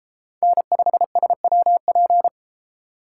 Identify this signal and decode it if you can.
Morse code, heard as a keyed tone, about 31 wpm, 700 Hz. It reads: N5SWP